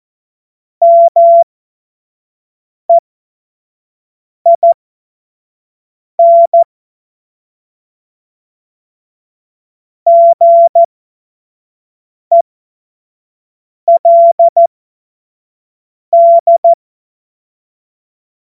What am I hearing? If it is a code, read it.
Morse code, heard as a keyed tone, about 14 wpm, spaced as for 5 wpm, 680 Hz. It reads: MEIN GELD
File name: Morsekurs / Witze / Witz 11. April 2023 25 F70